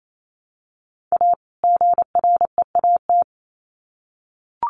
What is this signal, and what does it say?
Morse code, heard as a keyed tone, about 28 wpm, 700 Hz. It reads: A GREAT
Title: Sets of 2 Words 28wpm